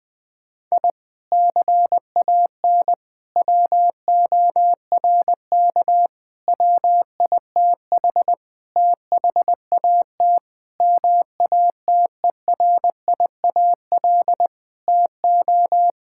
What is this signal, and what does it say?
Morse code, heard as a keyed tone, about 20 wpm, 700 Hz. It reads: I CAN WORK WITH THAT MATERIAL TO
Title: Sentences from Top 1000 Words - Review Part 5 - 20wpm